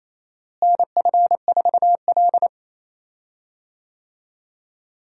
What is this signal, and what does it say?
Morse code, heard as a keyed tone, about 28 wpm, 700 Hz. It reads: NF4L
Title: USA Call Signs - 4 Characters - Repeated 3x - 28wpm